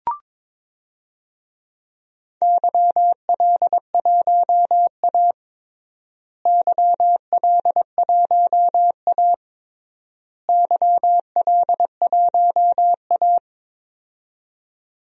Morse code, heard as a keyed tone, about 22 wpm, 700 Hz. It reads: YL1A YL1A YL1A
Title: DXCC-100 Call Signs - 4 Characters - Repeated 3x - 22wpm